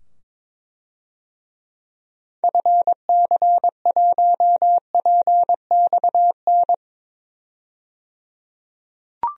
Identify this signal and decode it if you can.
Morse code, heard as a keyed tone, about 22 wpm, 700 Hz. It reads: FC1PXN
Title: DXCC-100 Call Signs - 6 Characters - 22wpm